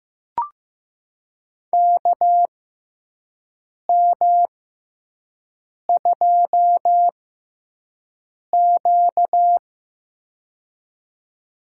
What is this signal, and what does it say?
Morse code, heard as a keyed tone, about 15 wpm, spaced as for 5 wpm, 700 Hz. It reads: KM2Q
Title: USA Call Signs - 6x Character Spacing - 15wpm